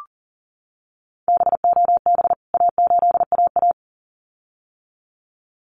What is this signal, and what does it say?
Morse code, heard as a keyed tone, about 40 wpm, 700 Hz. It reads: BOB A8AA
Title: CWT Exchange 40wpm